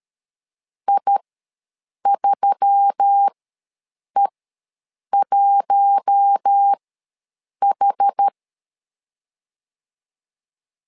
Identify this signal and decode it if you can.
Morse code, heard as a keyed tone, about 13 wpm, spaced as for 7 wpm, 800 Hz. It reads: I3E1H
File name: LEZIONI / 4 Q1S3 ws 12wpm 2038